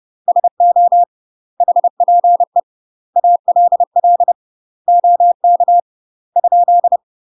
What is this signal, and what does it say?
Morse code, heard as a keyed tone, about 30 wpm, 700 Hz. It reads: SO HPE ALL OK ?
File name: UK MASTER QSOs TRAINING / 30WPM / UK MASTER QSOs 0020